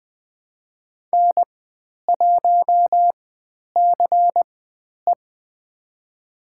Morse code, heard as a keyed tone, about 20 wpm, spaced as for 10 wpm, 700 Hz. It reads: N1CE
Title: 095 Lesson 16g - Letter c - Call Signs 20wpm